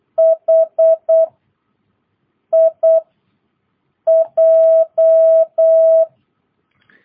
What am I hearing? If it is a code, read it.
Morse code, heard as a keyed tone, about 8 wpm, 650 Hz. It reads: H I J
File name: jh8cbh / 2022cw